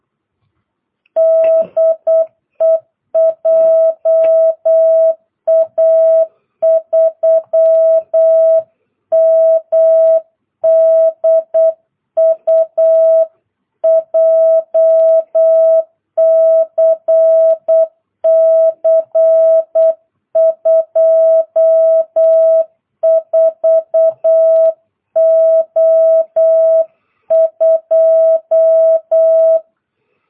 Morse code, heard as a keyed tone, about 8 wpm, 660 Hz. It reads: DE JA3MDU JCC 24O2